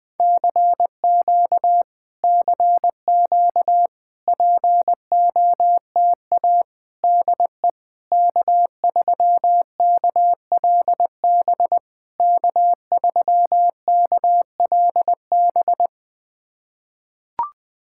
Morse code, heard as a keyed tone, about 20 wpm, 690 Hz. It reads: CQ CQ POTA DE K3KLB K3KLB
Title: POTA QSO Practice - 20wpm